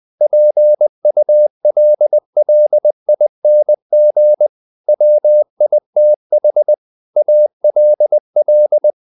Morse code, heard as a keyed tone, about 20 wpm, 600 Hz. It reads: PULLING WITH ALL
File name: the old man and the sea part 2 20wpm